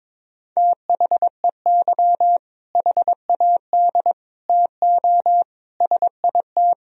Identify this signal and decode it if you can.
Morse code, heard as a keyed tone, about 22 wpm, 700 Hz. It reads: THEY HAD TO SIT
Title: Sentences from Top 1000 Words - Review Part 1 - 22wpm